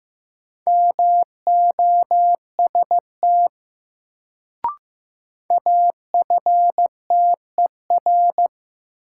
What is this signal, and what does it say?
Morse code, heard as a keyed tone, about 15 wpm, 700 Hz. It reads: MOST AFTER
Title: Sentences from Top 200 Words - New Word Drill - 15wpm